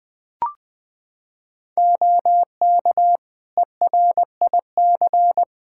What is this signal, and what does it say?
Morse code, heard as a keyed tone, about 20 wpm, 700 Hz. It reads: OK ERIC
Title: Practice QSO - Exchange 3 - More Info - 20wpm